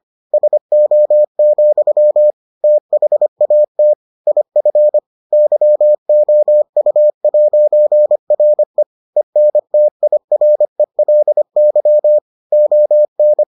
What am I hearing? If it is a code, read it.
Morse code, heard as a keyed tone, about 25 wpm, 600 Hz. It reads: SO, THAT IF YOU'RE ENTIRELY ON